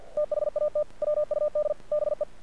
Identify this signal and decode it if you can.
Morse code, heard as a keyed tone, about 47 wpm, 600 Hz. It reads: THAT WUD BE
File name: archived / CRSnet